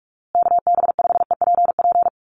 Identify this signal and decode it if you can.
Morse code, heard as a keyed tone, about 44 wpm, 700 Hz. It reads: KB5EPP
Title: USA Call Signs Repeated 2x 45wpm